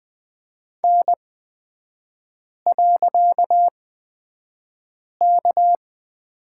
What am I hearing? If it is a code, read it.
Morse code, heard as a keyed tone, about 20 wpm, spaced as for 5 wpm, 700 Hz. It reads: N.K
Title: 255 Lesson 41e - Period - Three Characters 20wpm